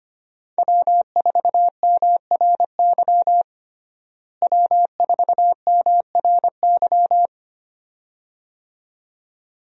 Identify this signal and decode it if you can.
Morse code, heard as a keyed tone, about 25 wpm, 700 Hz. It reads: W4MRY W4MRY
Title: USA Call Signs - 5 Characters - Repeated 2x - 25wpm